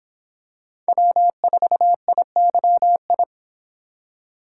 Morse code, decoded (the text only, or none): W4IYI